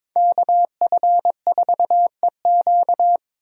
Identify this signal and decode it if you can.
Morse code, heard as a keyed tone, about 22 wpm, 700 Hz. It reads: KF4EQ
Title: USA Call Signs - 5 Characters - Repeated 2x - 22wpm